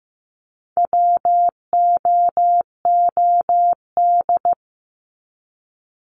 Morse code, heard as a keyed tone, about 15 wpm, 700 Hz. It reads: WOOD
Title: Top 300 Words 15wpm